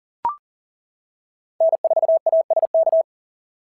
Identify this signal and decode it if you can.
Morse code, heard as a keyed tone, about 40 wpm, 650 Hz. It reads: N4ASK
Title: SOTA QSO Practice 40wpm